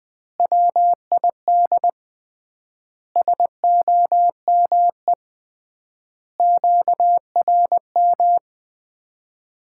Morse code, heard as a keyed tone, about 21 wpm, 700 Hz. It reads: WID SOME QRM
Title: Common QSO Phrases - 3x Word-Spacing - 20wpm